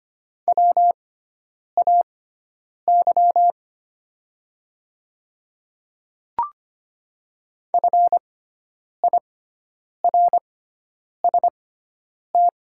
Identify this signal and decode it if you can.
Morse code, heard as a keyed tone, about 25 wpm, spaced as for 9 wpm, 700 Hz. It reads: WAY FIRST
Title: 6x Character Spacing - Top 100 Words - 25wpm